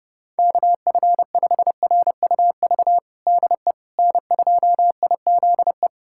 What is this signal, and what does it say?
Morse code, heard as a keyed tone, about 30 wpm, 700 Hz. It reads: KF5RUV DE N2IZE
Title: USA Callsign DE Callsign 30wpm